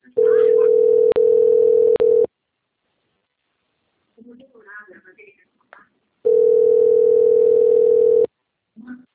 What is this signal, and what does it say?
Ringback tone (North America), 440 + 480 Hz, with speech.
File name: out- - - - - .22657